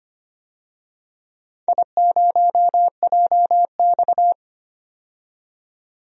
Morse code, heard as a keyed tone, about 25 wpm, 700 Hz. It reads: I0JX
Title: DXCC-100 Call Signs - 4 Characters - 25wpm